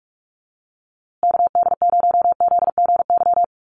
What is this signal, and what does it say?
Morse code, heard as a keyed tone, about 44 wpm, 700 Hz. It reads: KD0ZGY
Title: USA Call Signs Repeated 3x 45wpm